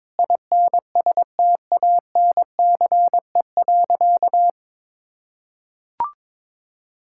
Morse code, heard as a keyed tone, about 22 wpm, 700 Hz. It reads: INSTANCE.
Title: Sentences from Top 1000 Words - Review Part 5 - 22wpm